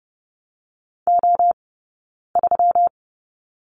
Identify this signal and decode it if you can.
Morse code, heard as a keyed tone, about 30 wpm, 700 Hz. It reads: O 3
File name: Lesson 20d - Number 3 - Two Characters 30wpm